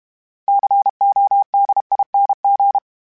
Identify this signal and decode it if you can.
Morse code, heard as a keyed tone, about 32 wpm, 800 Hz. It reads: CODING
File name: Special INGs-mp3s / Special INGs-mp3s / Sngl Ing @30